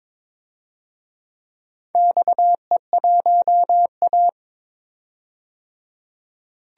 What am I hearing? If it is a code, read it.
Morse code, heard as a keyed tone, about 22 wpm, 700 Hz. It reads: XE1A
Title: DXCC-100 Call Signs - 4 Characters - 22wpm